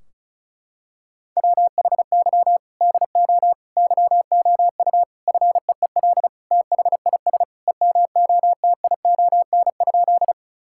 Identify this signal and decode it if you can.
Morse code, heard as a keyed tone, about 36 wpm, 700 Hz. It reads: WHY DO YOU FEEL THIS EMOTION?
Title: Sentences from Top 1600 Words 35wpm